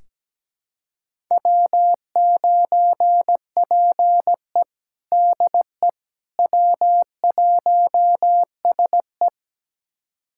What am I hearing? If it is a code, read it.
Morse code, heard as a keyed tone, about 17 wpm, 700 Hz. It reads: W9PE DE W1SE
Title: USA Callsign DE Callsign - 4 Characters - 17wpm